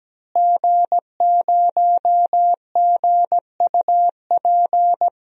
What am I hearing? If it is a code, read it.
Morse code, heard as a keyed tone, about 17 wpm, 700 Hz. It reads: G0GUP
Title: DXCC-100 Call Signs - 5 Characters - Repeated 3x - 17wpm